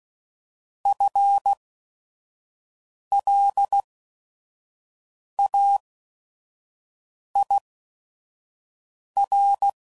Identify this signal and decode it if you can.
Morse code, heard as a keyed tone, about 16 wpm, spaced as for 5 wpm, 800 Hz. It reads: FLAIR
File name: Prefixes / By First Letter / Fs